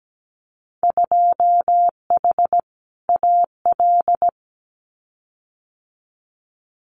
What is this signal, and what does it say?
Morse code, heard as a keyed tone, about 17 wpm, 700 Hz. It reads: 2H AL
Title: Winter Field Day Exchanges 17wpm